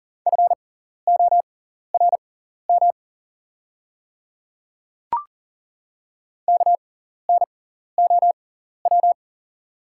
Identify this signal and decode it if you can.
Morse code, heard as a keyed tone, about 40 wpm, spaced as for 14 wpm, 700 Hz. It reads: FORM KNOW
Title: 6x Character Spacing - Top 300 Words - 40wpm